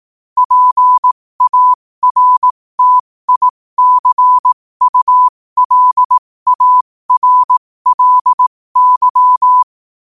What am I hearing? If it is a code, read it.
Morse code, heard as a keyed tone, about 18 wpm, spaced as for 15 wpm, 1000 Hz. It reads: PARTICULARLY
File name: qotdfiles / 15wpm